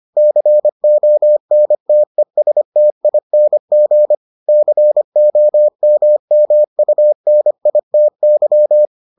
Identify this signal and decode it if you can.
Morse code, heard as a keyed tone, about 25 wpm, 600 Hz. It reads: CONTESTING COMMUNITY